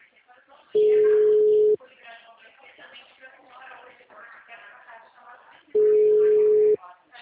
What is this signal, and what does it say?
Ringback tone (Europe, 425 Hz), with speech.
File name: out- -2012- - - .286